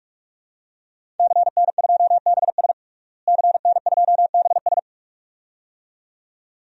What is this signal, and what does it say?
Morse code, heard as a keyed tone, about 46 wpm, 700 Hz. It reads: CN2BS CN2BS
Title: DXCC-100 Call Signs - 5 Characters - Repeated 2x - 45wpm